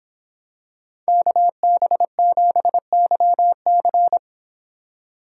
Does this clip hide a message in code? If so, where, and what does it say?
Yes: Morse code from 1.1 to 4.2 s, heard as a keyed tone, about 24 wpm, 700 Hz. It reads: KB7YC